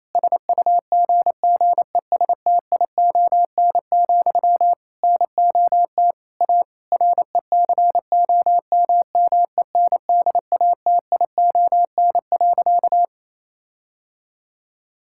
Morse code, heard as a keyed tone, about 28 wpm, 700 Hz. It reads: SUGGESTION, NOT A RECOMMENDATION.
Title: Sentences from Top 1600 Words 28wpm